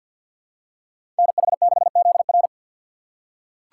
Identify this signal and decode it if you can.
Morse code, heard as a keyed tone, about 47 wpm, 700 Hz. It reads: NH6ZR
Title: USA Call Signs - 5 Characters - 4x Character Spacing - 50wpm